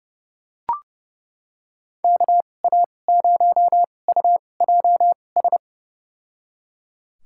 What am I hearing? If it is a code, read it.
Morse code, heard as a keyed tone, about 30 wpm, spaced as for 22 wpm, 700 Hz. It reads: KA0UJS
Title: USA Call Signs - 2x Character Spacing - 30wpm